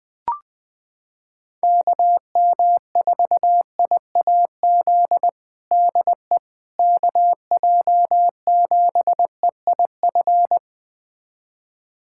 Morse code, heard as a keyed tone, about 20 wpm, 700 Hz. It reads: KM4IAZ DE KJ7EIF
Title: USA Callsign DE Callsign 20wpm